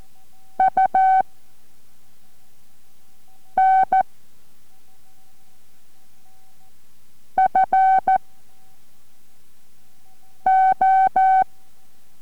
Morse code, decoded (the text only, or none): U N F O